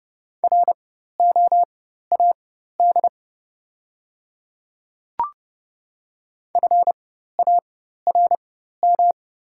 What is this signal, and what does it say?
Morse code, heard as a keyed tone, about 30 wpm, spaced as for 14 wpm, 690 Hz. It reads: ROAD FARM
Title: 4x Character Spacing - Top 500 Words - 30wpm